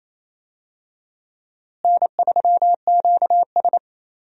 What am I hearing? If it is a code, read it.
Morse code, heard as a keyed tone, about 27 wpm, 700 Hz. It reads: N3QS
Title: USA Call Signs - 4 Characters - Repeated 3x - 28wpm